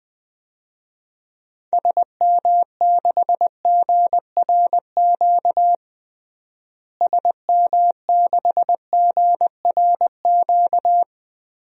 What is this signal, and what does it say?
Morse code, heard as a keyed tone, about 20 wpm, 700 Hz. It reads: SM6GRQ SM6GRQ
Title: DXCC-100 Call Signs - 6 Characters - Repeated 2x - 20wpm